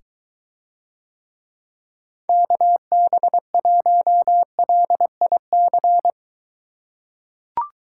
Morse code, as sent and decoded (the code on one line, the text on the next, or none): -.- -... .---- .-.. .. -.-.
KB1LIC